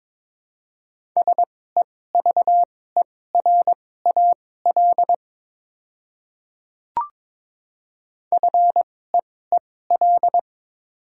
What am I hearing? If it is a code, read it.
Morse code, heard as a keyed tone, about 22 wpm, spaced as for 16 wpm, 700 Hz. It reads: SEVERAL FEEL
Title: 2x Character Spacing - Top 500 Words - 22wpm